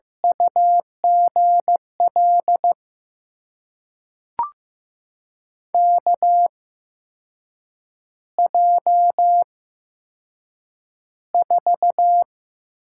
Morse code, heard as a keyed tone, about 15 wpm, spaced as for 6 wpm, 700 Hz. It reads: UGL KJ4
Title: USA Call Signs - 6 Characters - 8x Character Spacing - 15wpm